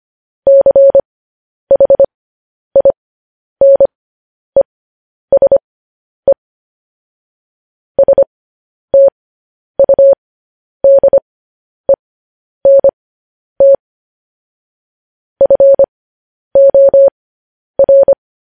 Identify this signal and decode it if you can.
Morse code, heard as a keyed tone, about 25 wpm, spaced as for 10 wpm, 550 Hz. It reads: CHINESE STUDENT FOR